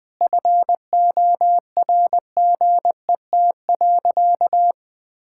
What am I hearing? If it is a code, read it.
Morse code, heard as a keyed tone, about 20 wpm, 700 Hz. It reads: FORGET.
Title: Sentences with 4-Word Phrases - 20wpm